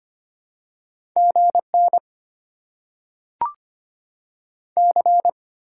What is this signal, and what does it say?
Morse code, heard as a keyed tone, about 25 wpm, 700 Hz. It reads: G N C
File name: Top 90 QSO Elements - 6x Character Spacing - 25wpm